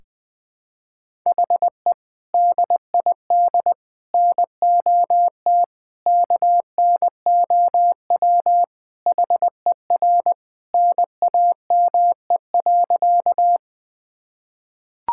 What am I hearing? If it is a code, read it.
Morse code, heard as a keyed tone, about 21 wpm, 700 Hz. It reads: HE DID NOT KNOW HER NAME.
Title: Sentences with 4-Word Phrases - 20wpm